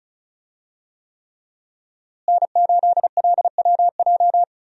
Morse code, heard as a keyed tone, about 36 wpm, 700 Hz. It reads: N8LWJ
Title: USA Call Signs - 5 Characters - Repeated 3x - 35wpm